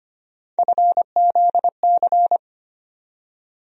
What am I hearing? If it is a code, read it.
Morse code, heard as a keyed tone, about 25 wpm, 700 Hz. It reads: FZC
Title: USA Call Sign Postfixes - 3 Letters 25wpm